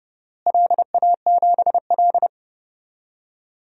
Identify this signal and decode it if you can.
Morse code, heard as a keyed tone, about 30 wpm, 700 Hz. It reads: LA7L
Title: DXCC-100 Call Signs - 4 Characters - Repeated 2x - 30wpm